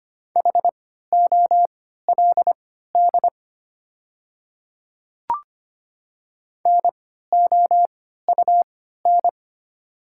Morse code, heard as a keyed tone, about 25 wpm, spaced as for 14 wpm, 700 Hz. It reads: HOLD NOUN
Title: 3x Character Spacing - Top 500 Words - 25wpm